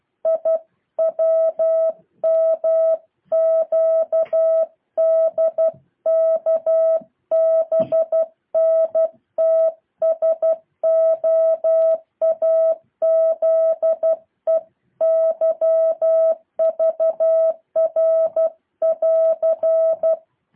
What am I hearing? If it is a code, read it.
Morse code, heard as a keyed tone, about 12 wpm, 650 Hz. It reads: IWMQDKBNTSOAZEYVR<AR>
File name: jh8cbh / 2023cw